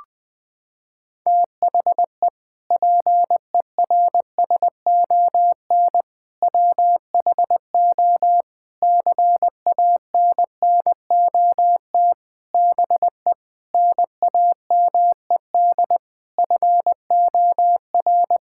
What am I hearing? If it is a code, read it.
Morse code, heard as a keyed tone, about 20 wpm, 700 Hz. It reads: THE PERSON WHO CANNOT BE NAMED FOR
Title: Sentences with 8-Word Phrases - 20wpm